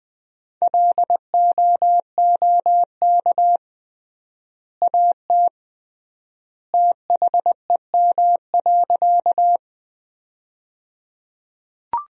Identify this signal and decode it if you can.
Morse code, heard as a keyed tone, about 21 wpm, 700 Hz. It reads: LOOK AT THEM.
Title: 3x Word-Spacing - Sentences from Top 100 Words - 20wpm